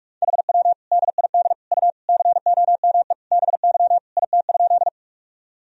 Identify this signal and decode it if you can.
Morse code, heard as a keyed tone, about 42 wpm, 700 Hz. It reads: HW DID U COME BY IT?
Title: Sentences from Top 100 Words and Abbreviations 45wpm